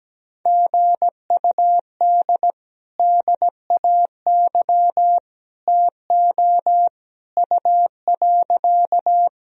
Morse code, heard as a keyed tone, about 17 wpm, 700 Hz. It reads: GUD DAY TO U.
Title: Sentences from Top 200 Words and Abbreviations 17wpm